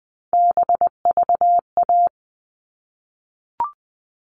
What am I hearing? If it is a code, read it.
Morse code, heard as a keyed tone, about 20 wpm, 700 Hz. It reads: BVA
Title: All Cut Numbers - 3-Digit 20wpm